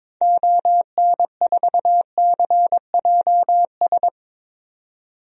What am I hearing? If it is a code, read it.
Morse code, heard as a keyed tone, about 22 wpm, 700 Hz. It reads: ON4CJS
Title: DXCC-100 Call Signs - 6 Characters - 22wpm